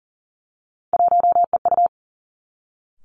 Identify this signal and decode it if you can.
Morse code, heard as a keyed tone, about 40 wpm, 700 Hz. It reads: 1EU
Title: Three Challenging Letter-Number Combinations 40wpm